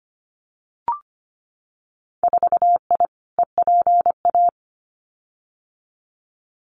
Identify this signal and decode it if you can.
Morse code, heard as a keyed tone, about 25 wpm, 700 Hz. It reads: 4I EPA
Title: Winter Field Day Exchanges 25wpm